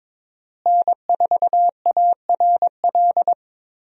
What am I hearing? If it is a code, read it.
Morse code, heard as a keyed tone, about 22 wpm, 700 Hz. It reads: N4ARL
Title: USA Call Signs - 5 Characters - Repeated 2x - 22wpm